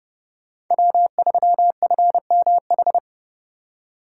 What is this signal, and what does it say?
Morse code, heard as a keyed tone, about 29 wpm, 700 Hz. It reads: W3FMH